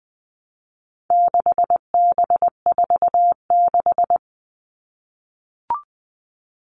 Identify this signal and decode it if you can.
Morse code, heard as a keyed tone, about 20 wpm, 700 Hz. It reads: 6B46
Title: All Cut Numbers - 4-Digit 20wpm